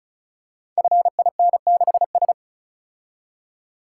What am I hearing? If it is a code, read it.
Morse code, heard as a keyed tone, about 36 wpm, 700 Hz. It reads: FIN6S
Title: Code Groups - 5 Letter:Numbers - 35wpm